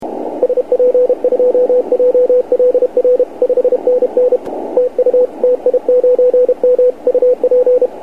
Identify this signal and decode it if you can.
Morse code, heard as a keyed tone, about 32 wpm, 500 Hz, with radio noise. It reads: SP2JLR 5NN TU TI9M UP
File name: dxrecords / ti9m 28cw